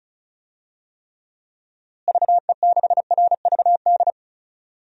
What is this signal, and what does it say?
Morse code, heard as a keyed tone, about 34 wpm, 700 Hz. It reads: VE6RVD
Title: DXCC-100 Call Signs - 6 Characters - Repeated 3x - 35wpm